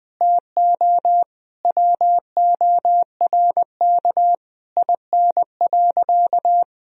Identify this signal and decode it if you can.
Morse code, heard as a keyed tone, about 20 wpm, 700 Hz. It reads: TO WORK IN.
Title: Sentences from Top 1000 Words - Review Part 4 - 20wpm